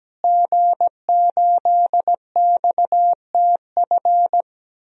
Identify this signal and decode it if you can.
Morse code, heard as a keyed tone, about 17 wpm, 700 Hz. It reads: G8XTF
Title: DXCC-100 Call Signs - 5 Characters - Repeated 3x - 17wpm